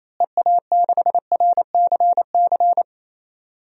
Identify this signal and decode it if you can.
Morse code, heard as a keyed tone, about 28 wpm, 700 Hz. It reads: EA6RCC
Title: DXCC-100 Call Signs - 6 Characters - Repeated 3x - 28wpm